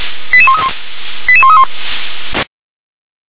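Five-tone selective call (ZVEI), heard as quick runs of five tones, tones of about 70 ms. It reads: 88131 88131
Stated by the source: ZVEI five-tone call 88150 received by SDR.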